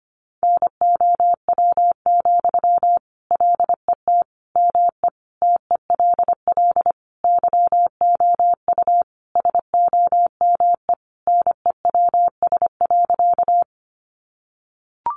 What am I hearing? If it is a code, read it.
Morse code, heard as a keyed tone, about 25 wpm, 700 Hz. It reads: NOW, LET ME TELL YOU SOME NEWS.
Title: Sentences from Top 800 Words 25wpm